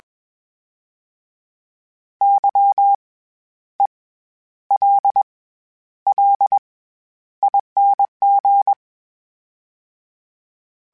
Morse code, heard as a keyed tone, about 21 wpm, spaced as for 11 wpm, 800 Hz. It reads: YELLING
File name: Special INGs-mp3s / Special INGs-mp3s / SlowToFaster ing @20